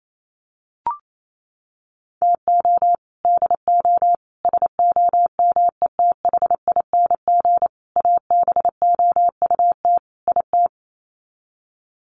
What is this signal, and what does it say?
Morse code, heard as a keyed tone, about 28 wpm, 700 Hz. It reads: TO DO SOMETHING ABOUT IT
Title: Sets of 5 Words 28wpm